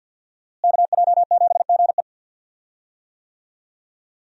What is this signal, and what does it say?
Morse code, heard as a keyed tone, about 51 wpm, 700 Hz. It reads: KJ7GE
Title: USA Call Signs - 5 Characters - Repeated 3x - 50wpm